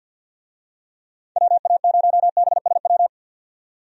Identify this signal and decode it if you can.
Morse code, heard as a keyed tone, about 51 wpm, 700 Hz. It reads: WA0BSW
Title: USA Call Signs - 6 Characters - 50wpm